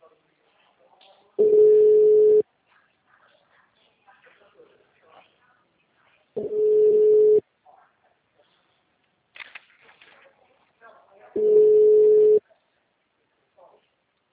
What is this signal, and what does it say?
Ringback tone (Europe, 425 Hz), with speech.